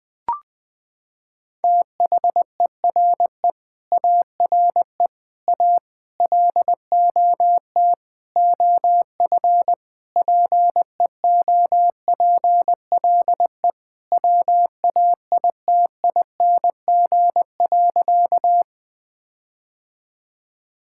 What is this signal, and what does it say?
Morse code, heard as a keyed tone, about 20 wpm, 700 Hz. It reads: THERE ARE A LOT OF PEOPLE WAITING.
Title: Sentences with 5-Word Phrases - 20wpm